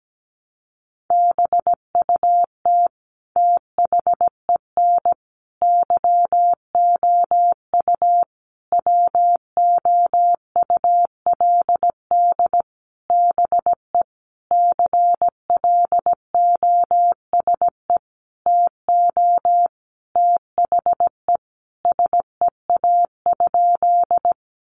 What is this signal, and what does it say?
Morse code, heard as a keyed tone, about 17 wpm, 700 Hz. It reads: BUT THEN YOU WOULD BE CLOSE TO THE SEA?
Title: Sentences from Top 300 Words 17wpm